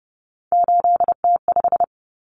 Morse code, heard as a keyed tone, about 30 wpm, 700 Hz. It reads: 8T5
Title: Common Cut Numbers - 3-Digit 30wpm